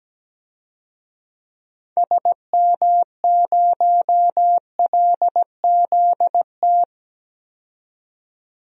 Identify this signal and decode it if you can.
Morse code, heard as a keyed tone, about 17 wpm, 700 Hz. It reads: SM0LZT
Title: DXCC-100 Call Signs - 6 Characters - 17wpm